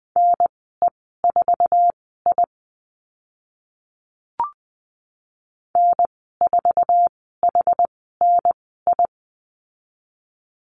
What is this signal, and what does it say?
Morse code, heard as a keyed tone, about 20 wpm, spaced as for 14 wpm, 700 Hz. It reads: NE4I N4HNI
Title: 062 Lesson 11g - Letter h - Call Signs 20wpm